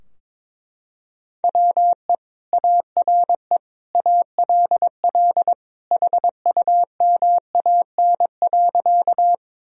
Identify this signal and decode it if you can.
Morse code, heard as a keyed tone, about 22 wpm, 700 Hz. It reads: WE ARE ALL HUMAN.
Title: Sentences from Top 1000 Words - Review Part 3 - 22wpm